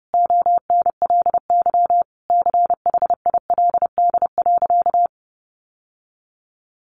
Morse code, heard as a keyed tone, about 30 wpm, 700 Hz. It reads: ONLY CHILD.
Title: Sentences from Top 800 Words 30wpm